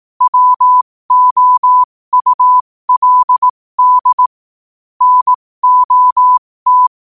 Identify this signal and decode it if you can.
Morse code, heard as a keyed tone, about 18 wpm, spaced as for 15 wpm, 1000 Hz. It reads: WOULD NOT
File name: qotdfiles / 15wpm